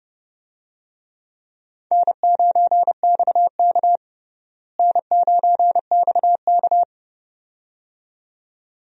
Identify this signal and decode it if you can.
Morse code, heard as a keyed tone, about 30 wpm, 700 Hz. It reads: N9XK N9XK
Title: USA Call Signs - 4 Characters - Repeated 2x - 30wpm